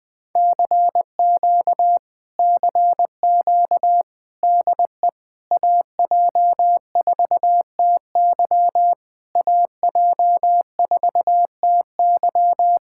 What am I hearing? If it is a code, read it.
Morse code, heard as a keyed tone, about 20 wpm, 700 Hz. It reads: CQ CQ DE AJ4TY AJ4TY
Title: Speed-Racing Common QSO Phrases - 20wpm